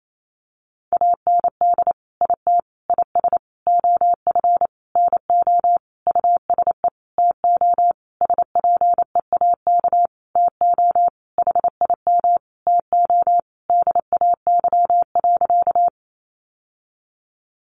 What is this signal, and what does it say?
Morse code, heard as a keyed tone, about 28 wpm, 700 Hz. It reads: AND IT IS OF NO USE TO SPEAK TO HIM TO DAY.
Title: Sentences from Top 1100 Words 28wpm